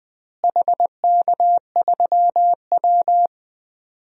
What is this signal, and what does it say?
Morse code, heard as a keyed tone, about 20 wpm, 700 Hz. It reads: HK3W
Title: DXCC-100 Call Signs - 4 Characters - Repeated 3x - 20wpm